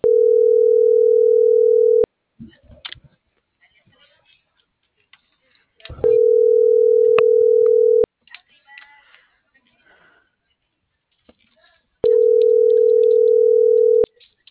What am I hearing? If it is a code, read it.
Ringback tone (North America), 440 + 480 Hz.